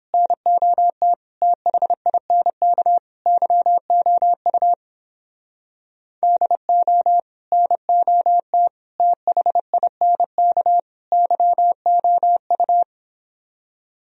Morse code, heard as a keyed tone, about 29 wpm, 700 Hz. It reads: NOT THINK YOU DO NOT THINK YOU